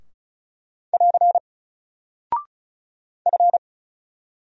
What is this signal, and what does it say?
Morse code, heard as a keyed tone, about 34 wpm, 700 Hz. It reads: <AR> F
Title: Single Character (Letter:Number:Punctuation:Prosign) 35wpm